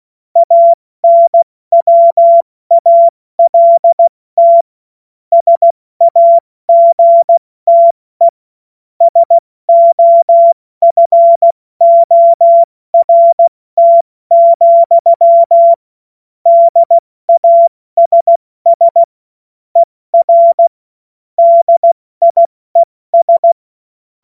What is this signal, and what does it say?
Morse code, heard as a keyed tone, about 16 wpm, 680 Hz. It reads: ANWALT SAGTE SOFORT, DASS ER DIES
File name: Morsekurs / Witze / Witz 25. Oktober 2022 70 F80